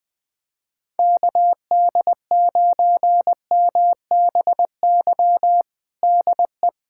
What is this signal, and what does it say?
Morse code, heard as a keyed tone, about 21 wpm, 700 Hz. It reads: KD9MBY DE
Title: USA Callsign DE Callsign 20wpm